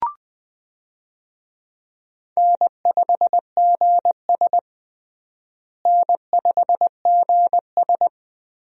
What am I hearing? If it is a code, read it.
Morse code, heard as a keyed tone, about 21 wpm, 700 Hz. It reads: N5GS N5GS
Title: USA Call Signs - 4 Characters - Repeated 2x - 20wpm